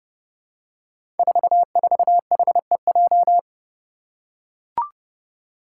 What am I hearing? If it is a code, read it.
Morse code, heard as a keyed tone, about 30 wpm, 700 Hz. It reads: 44HEJ
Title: Five Challenging Letter-Number Combinations 30wpm